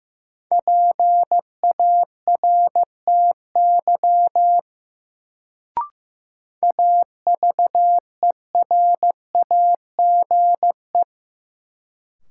Morse code, heard as a keyed tone, about 15 wpm, 700 Hz. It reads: PARTY AVERAGE
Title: Sentences from Top 1400 Words - New Word Drill Alt - 15wpm